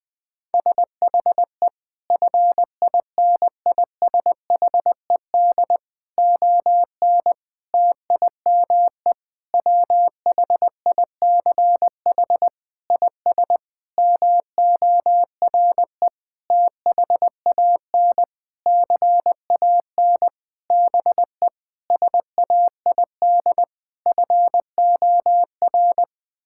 Morse code, heard as a keyed tone, about 20 wpm, 700 Hz. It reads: SHE FINISHED ON TIME WHICH IS MORE THAN CAN BE SAID FOR
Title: Sentences with 8-Word Phrases - 20wpm